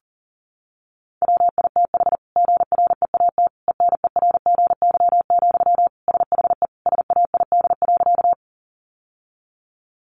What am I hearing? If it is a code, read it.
Morse code, heard as a keyed tone, about 40 wpm, 700 Hz. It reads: WITH GREAT ENERGY, SHE SAID.
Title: Sentences from Top 900 Words 40wpm